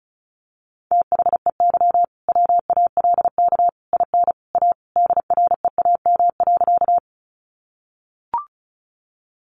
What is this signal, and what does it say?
Morse code, heard as a keyed tone, about 34 wpm, 700 Hz. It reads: THEY WALK IN A DREAM.
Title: Sentences from Top 1500 Words 35wpm